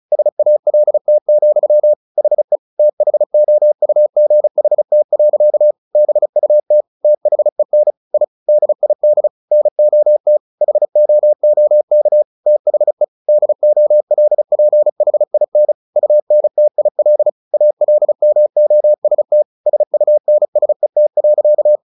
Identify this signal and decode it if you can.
Morse code, heard as a keyed tone, about 34 wpm, 600 Hz. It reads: SALT, HE THOUGHT. BUT THEN I DID NOT HOOK THE DOLPHIN UNTIL ALMOST SUNSET.
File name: the old man and the sea 2 35wpm